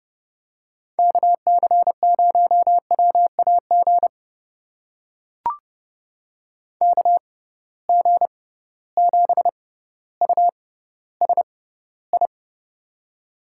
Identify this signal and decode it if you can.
Morse code, heard as a keyed tone, about 30 wpm, spaced as for 15 wpm, 700 Hz. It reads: KC0WAG KG7USI
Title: USA Call Signs - 6 Characters - 6x Character Spacing - 30wpm